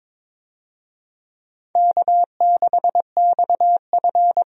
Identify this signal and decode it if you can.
Morse code, heard as a keyed tone, about 22 wpm, 700 Hz. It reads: K6XF
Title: USA Call Signs - 4 Characters - Repeated 2x - 22wpm